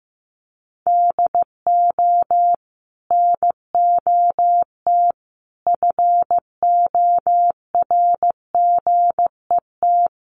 Morse code, heard as a keyed tone, about 15 wpm, 700 Hz. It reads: DO NOT FORGET
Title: Sentences from Top 1200 Words 15wpm